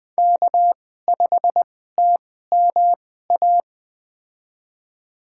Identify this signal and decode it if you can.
Morse code, heard as a keyed tone, about 20 wpm, spaced as for 14 wpm, 700 Hz. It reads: K5TMA
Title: USA Call Signs - 5 Characters - 2x Character Spacing - 20wpm